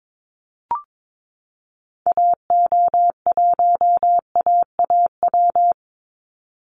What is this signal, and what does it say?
Morse code, heard as a keyed tone, about 22 wpm, 700 Hz. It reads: AO1AAW
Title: Call Signs Repeated 3x - 22wpm